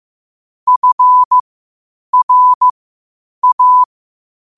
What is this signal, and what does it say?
Morse code, heard as a keyed tone, about 15 wpm, spaced as for 8 wpm, 1000 Hz. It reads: FRA
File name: qotdfiles / 8wpm